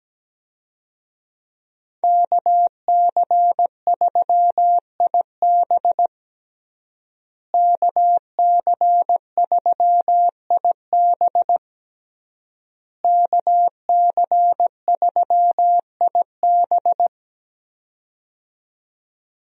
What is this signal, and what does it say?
Morse code, heard as a keyed tone, about 17 wpm, 700 Hz. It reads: KC3IB KC3IB KC3IB
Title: USA Call Signs - 5 Characters - Repeated 3x - 17wpm